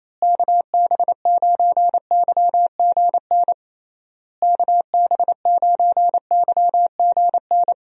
Morse code, heard as a keyed tone, about 27 wpm, 700 Hz. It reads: KB9YGN KB9YGN
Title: USA Call Signs - 6 Characters - Repeated 3x - 28wpm